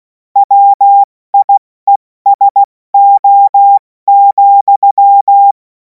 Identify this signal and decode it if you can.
Morse code, heard as a keyed tone, about 16 wpm, 800 Hz. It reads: WIESO,
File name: Morsekurs / Witze / Witz 21. April 2026 70 F80